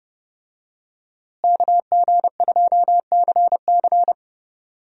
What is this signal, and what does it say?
Morse code, heard as a keyed tone, about 31 wpm, 700 Hz. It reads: KG2CC
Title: USA Call Signs - 5 Characters - Repeated 2x - 30wpm